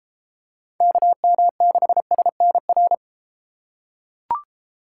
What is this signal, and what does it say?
Morse code, heard as a keyed tone, about 32 wpm, 700 Hz. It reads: KM6SNR